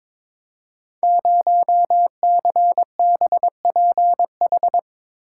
Morse code, heard as a keyed tone, about 22 wpm, 700 Hz. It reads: 0CBPH
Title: Code Groups - 5 Letter:Numbers - 22wpm